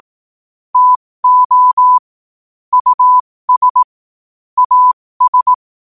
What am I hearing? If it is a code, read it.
Morse code, heard as a keyed tone, about 18 wpm, spaced as for 15 wpm, 1000 Hz. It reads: TO US AS
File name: qotdfiles / 15wpm